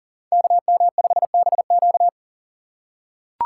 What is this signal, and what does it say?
Morse code, heard as a keyed tone, about 40 wpm, 700 Hz. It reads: KM5BQ
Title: USA Call Signs - 5 Characters - 3x Character Spacing - 40wpm